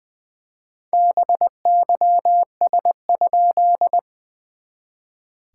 Morse code, heard as a keyed tone, about 21 wpm, 690 Hz. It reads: BYS?
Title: SOTA QSO Practice 20wpm